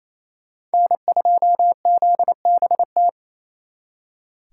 Morse code, heard as a keyed tone, about 36 wpm, 700 Hz. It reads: N2ZBT